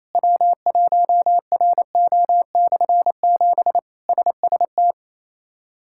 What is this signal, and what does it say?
Morse code, heard as a keyed tone, about 28 wpm, 700 Hz. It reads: W1RO/7 SST
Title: K1USN SST QSO Practice - 28wpm